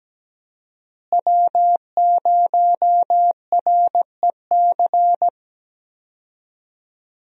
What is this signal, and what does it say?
Morse code, heard as a keyed tone, about 17 wpm, 700 Hz. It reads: W0REC
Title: USA Call Signs - 5 Characters - Repeated 3x - 17wpm